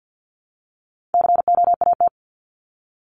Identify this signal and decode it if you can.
Morse code, heard as a keyed tone, about 48 wpm, 700 Hz. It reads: COAT
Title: Top 1501-1750 Words 50wpm